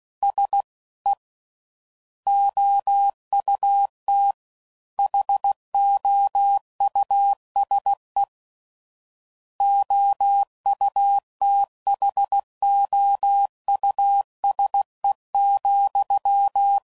Morse code, heard as a keyed tone, about 16 wpm, 800 Hz. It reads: S E OUT HOUSE OUTHOUSE,
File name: Suffixes / General / Houses